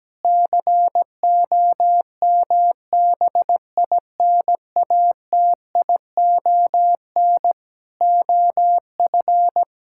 Morse code, heard as a keyed tone, about 17 wpm, 700 Hz. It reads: COMBINATION OF
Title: Sentences from Top 1000 Words - Review Part 6 - 17wpm